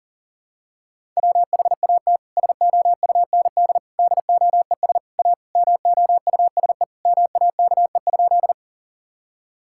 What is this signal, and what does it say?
Morse code, heard as a keyed tone, about 40 wpm, 700 Hz. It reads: WHAT SOUND DOES A MOUSE MAKE?
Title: Sentences from Top 1000 Words - Review Part 5 - 40wpm